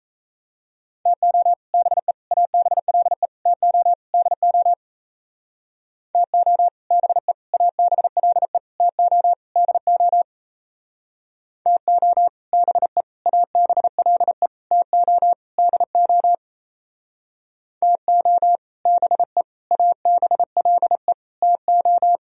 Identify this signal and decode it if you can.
Morse code, heard as a keyed tone, about 34 wpm, 700 Hz. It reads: TO BE ABLE TO DO TO BE ABLE TO DO TO BE ABLE TO DO TO BE ABLE TO
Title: Speed-Racing - Sets of 5 Words 28wpm